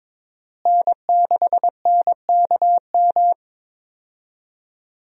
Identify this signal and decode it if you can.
Morse code, heard as a keyed tone, about 22 wpm, 700 Hz. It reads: N6NKM
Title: USA Call Signs - 5 Characters - Repeated 2x - 22wpm